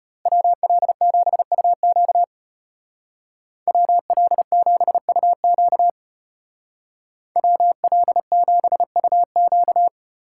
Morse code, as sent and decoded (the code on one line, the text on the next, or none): .-- .-.. --... ..- --.- / .-- .-.. --... ..- --.- / .-- .-.. --... ..- --.-
WL7UQ WL7UQ WL7UQ